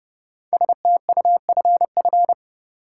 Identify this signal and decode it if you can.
Morse code, heard as a keyed tone, about 30 wpm, 700 Hz. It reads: STUFF
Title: Sentences from Top 1400 Words - New Word Drill - 30wpm